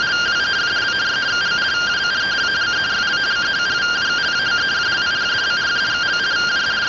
Two-tone teleprinter signal, about 170 Hz shift.